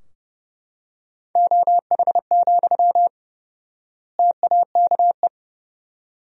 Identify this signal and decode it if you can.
Morse code, heard as a keyed tone, about 30 wpm, 700 Hz. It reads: OH, TAKE
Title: 4x Word-Spacing - Sentences from Top 500 Words - 30wpm